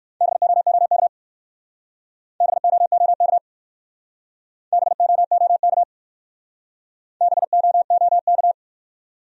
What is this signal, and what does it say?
Morse code, heard as a keyed tone, about 53 wpm, 700 Hz. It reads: BOOK BOOK BOOK BOOK
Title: Speed-Racing Top 300 Words 45wpm